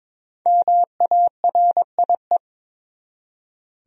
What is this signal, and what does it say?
Morse code, heard as a keyed tone, about 22 wpm, 700 Hz. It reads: MARIE
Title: USA Common Names - Max 5 Letters - 22wpm